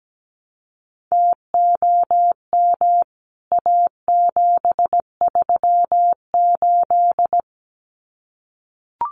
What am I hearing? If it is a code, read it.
Morse code, heard as a keyed tone, about 17 wpm, 700 Hz. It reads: TOM A738
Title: CWT Exchange 17wpm